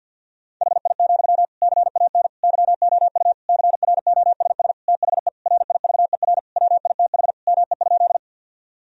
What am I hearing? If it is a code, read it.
Morse code, heard as a keyed tone, about 49 wpm, 700 Hz. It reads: HI, CAN YOU CROSS THE RIVER WITH ME?
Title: Sentences from Top 1000 Words - Review Part 3 - 50wpm